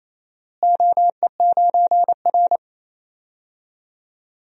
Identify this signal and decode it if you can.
Morse code, heard as a keyed tone, about 28 wpm, 700 Hz. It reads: OE9R
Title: DXCC-100 Call Signs - 4 Characters - Repeated 3x - 28wpm